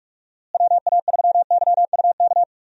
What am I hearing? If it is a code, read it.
Morse code, heard as a keyed tone, about 46 wpm, 700 Hz. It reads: WA3YUK
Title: USA Call Signs - 6 Characters - Repeated 2x - 45wpm